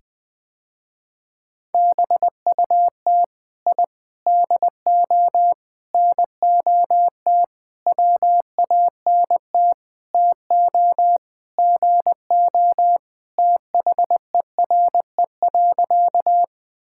Morse code, heard as a keyed tone, about 20 wpm, 700 Hz. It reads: BUT I DO NOT WANT TO GO THERE.
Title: Sentences with 5-Word Phrases - Spoken Beginning - 20wpm